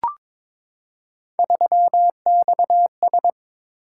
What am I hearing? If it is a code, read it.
Morse code, heard as a keyed tone, about 22 wpm, 700 Hz. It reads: 3XS
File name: practice / 22wpm / Three Letter-Number Combinations - 22wpm